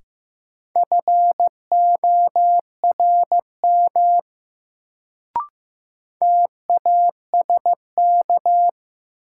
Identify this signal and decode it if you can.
Morse code, heard as a keyed tone, about 15 wpm, 700 Hz. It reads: FORM TASK
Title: Sentences from Top 1100 Words - New Word Drill Alt - 15wpm